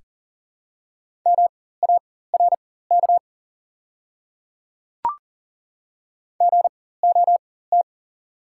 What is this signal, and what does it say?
Morse code, heard as a keyed tone, about 41 wpm, spaced as for 19 wpm, 690 Hz. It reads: MARK GOT
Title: 4x Character Spacing - Top 300 Words - 40wpm